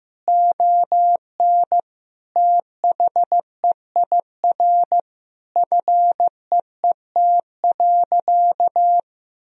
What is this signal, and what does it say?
Morse code, heard as a keyed tone, about 16 wpm, 700 Hz. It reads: ON THEIR FEET.
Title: Sentences from Top 1000 Words - Review Part 1 - 15wpm